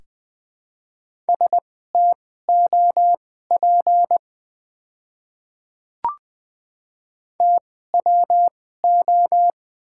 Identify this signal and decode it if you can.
Morse code, heard as a keyed tone, about 20 wpm, spaced as for 15 wpm, 700 Hz. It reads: STOP TWO
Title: 2x Character Spacing - Top 300 Words - 20wpm